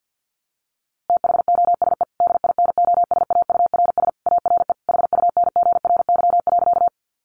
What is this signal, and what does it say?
Morse code, heard as a keyed tone, about 50 wpm, 700 Hz. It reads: THOSE DINOSAURS ARE HUNGRY.
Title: A Sound Of Thunder 50wpm